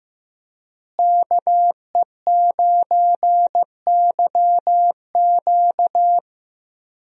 Morse code, heard as a keyed tone, about 15 wpm, 700 Hz. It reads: KE9YQ
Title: USA Call Signs - 5 Characters - Repeated 2x - 15wpm